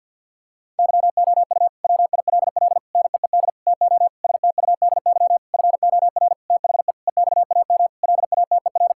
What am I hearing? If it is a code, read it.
Morse code, heard as a keyed tone, about 50 wpm, 700 Hz. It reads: YOU WILL NEED TO STUDY FOR THE EXAM LATER
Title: Sentences from Top 1000 Words - Review Part 5 - 50wpm